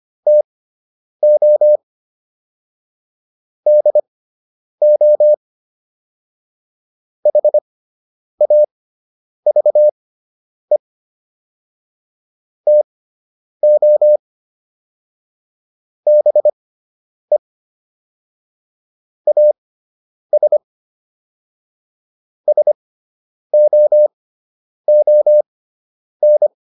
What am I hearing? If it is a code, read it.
Morse code, heard as a keyed tone, about 25 wpm, spaced as for 9 wpm, 600 Hz. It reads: TO DO HAVE TO BE AS SOON